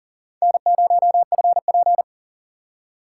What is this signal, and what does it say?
Morse code, heard as a keyed tone, about 40 wpm, 700 Hz. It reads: N0FP
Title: USA Call Signs - 4 Characters - 3x Character Spacing - 40wpm